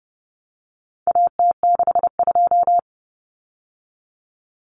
Morse code, heard as a keyed tone, about 30 wpm, 700 Hz. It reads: AT62
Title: Common Cut Numbers - 4-Digit 30wpm